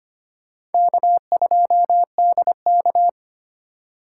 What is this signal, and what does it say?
Morse code, heard as a keyed tone, about 34 wpm, 700 Hz. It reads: K2DK